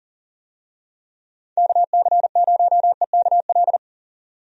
Morse code, heard as a keyed tone, about 39 wpm, 700 Hz. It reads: KC0EKL
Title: USA Call Signs - 6 Characters - 40wpm